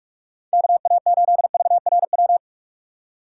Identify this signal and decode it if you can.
Morse code, heard as a keyed tone, about 46 wpm, 700 Hz. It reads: KA8VRW
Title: USA Call Signs - 6 Characters - 6x Character Spacing - 45wpm